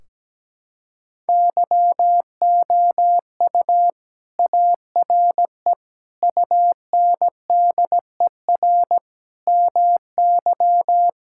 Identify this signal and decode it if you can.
Morse code, heard as a keyed tone, about 17 wpm, 700 Hz. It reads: YOU ARE UNDER MY
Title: Sentences from Top 1600 Words 17wpm